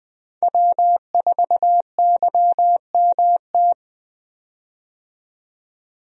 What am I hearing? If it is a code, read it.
Morse code, heard as a keyed tone, about 21 wpm, 700 Hz. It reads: W4YMT
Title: USA Call Signs - 5 Characters - 20wpm